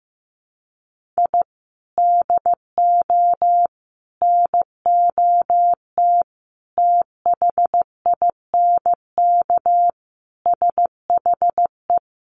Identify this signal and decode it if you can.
Morse code, heard as a keyed tone, about 15 wpm, 700 Hz. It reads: I DO NOT THINK SHE
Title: Sets of 5 Words 15wpm